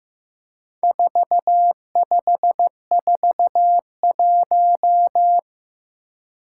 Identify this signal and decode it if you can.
Morse code, heard as a keyed tone, about 15 wpm, 700 Hz. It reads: 4541
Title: Four Challenging Letter-Number Combinations 15wpm